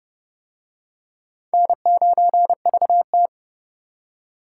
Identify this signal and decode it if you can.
Morse code, heard as a keyed tone, about 28 wpm, 700 Hz. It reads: N9VT